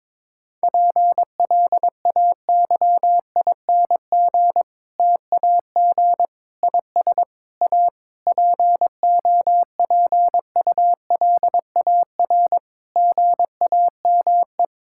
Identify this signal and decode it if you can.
Morse code, heard as a keyed tone, about 22 wpm, 700 Hz. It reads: PLAYING TAG IS A POPULAR GAME
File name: Difficult Sentences with Hints - 22wpm